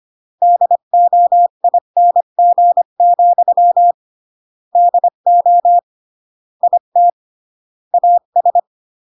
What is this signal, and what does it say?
Morse code, heard as a keyed tone, about 25 wpm, 700 Hz. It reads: DOING, DO IT AS
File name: CW / jodorowsky / 82 commandments 25-20